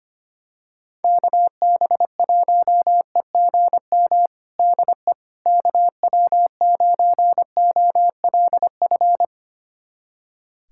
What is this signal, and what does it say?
Morse code, heard as a keyed tone, about 25 wpm, 700 Hz. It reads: KB1EGM DE KW9OLF
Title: USA Callsign DE Callsign 25wpm